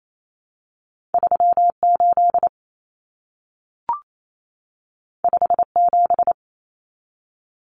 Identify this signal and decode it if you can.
Morse code, heard as a keyed tone, about 28 wpm, 700 Hz. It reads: 38 57
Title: Numbers - 2-Digit 28wpm